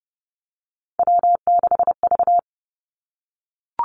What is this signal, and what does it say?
Morse code, heard as a keyed tone, about 30 wpm, 700 Hz. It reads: W6V
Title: Three Challenging Letter-Number Combinations 30wpm